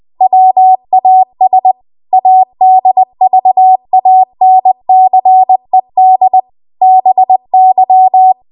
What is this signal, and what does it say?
Morse code, heard as a keyed tone, about 20 wpm, 750 Hz. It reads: WAS ADVANCED BY
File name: Morse / 20WPM